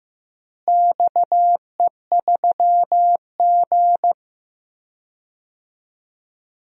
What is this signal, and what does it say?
Morse code, heard as a keyed tone, about 15 wpm, 700 Hz. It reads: XE3G
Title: DXCC-100 Call Signs - 4 Characters - Repeated 3x - 15wpm